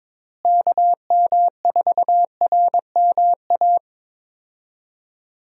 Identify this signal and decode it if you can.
Morse code, heard as a keyed tone, about 22 wpm, 700 Hz. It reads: KM4RMA
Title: USA Call Signs - 6 Characters - Repeated 2x - 22wpm